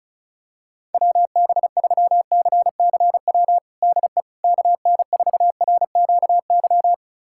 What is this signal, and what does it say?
Morse code, heard as a keyed tone, about 36 wpm, 700 Hz. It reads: WB3CCW DE KN4RQY
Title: USA Callsign DE Callsign - 6 Characters - 35wpm